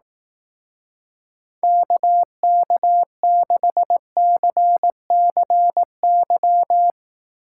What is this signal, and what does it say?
Morse code, heard as a keyed tone, about 18 wpm, 700 Hz. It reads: KK6CCY
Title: Speed Racing - USA 2x3 Call Signs 15wpm